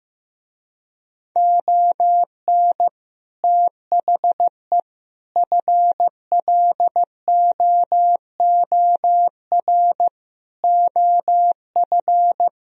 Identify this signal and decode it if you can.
Morse code, heard as a keyed tone, about 16 wpm, 700 Hz. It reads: ON THE FLOOR OF
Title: Speed-Racing - Sets of 5 Words 15wpm